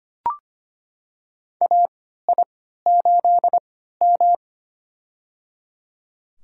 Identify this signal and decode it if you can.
Morse code, heard as a keyed tone, about 25 wpm, spaced as for 14 wpm, 700 Hz. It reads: AI8M
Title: USA Call Signs - 4 Characters - 3x Character Spacing - 25wpm